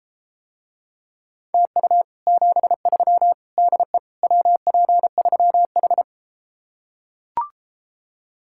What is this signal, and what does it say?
Morse code, heard as a keyed tone, about 33 wpm, 700 Hz. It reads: TU 73 DE WP3H